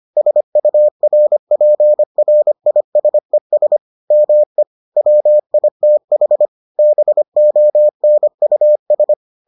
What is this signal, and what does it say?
Morse code, heard as a keyed tone, about 25 wpm, 600 Hz. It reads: SURPRISES ME WITH BONUS